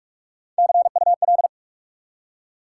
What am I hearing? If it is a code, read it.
Morse code, heard as a keyed tone, about 44 wpm, 700 Hz. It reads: CUL